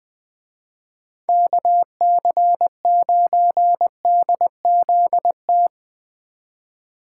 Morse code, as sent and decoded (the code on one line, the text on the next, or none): -.- -.-. ----. -.. --.. -
KC9DZT